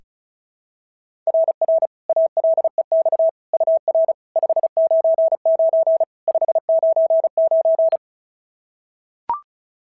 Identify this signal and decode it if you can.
Morse code, heard as a keyed tone, about 33 wpm, 650 Hz. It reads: RR ALEX UR 599 599
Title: SOTA QSO Practice 35wpm